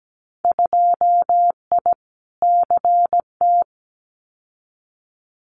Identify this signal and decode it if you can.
Morse code, heard as a keyed tone, about 17 wpm, 700 Hz. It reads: 2I CT
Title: Winter Field Day Exchanges 17wpm